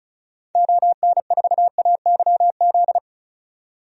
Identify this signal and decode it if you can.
Morse code, heard as a keyed tone, about 36 wpm, 700 Hz. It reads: ON4AYZ
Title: DXCC-100 Call Signs - 6 Characters - 35wpm